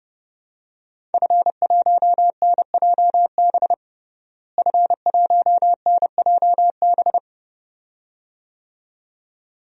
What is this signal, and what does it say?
Morse code, heard as a keyed tone, about 30 wpm, 700 Hz. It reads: F1NJB F1NJB
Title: DXCC-100 Call Signs - 5 Characters - Repeated 2x - 30wpm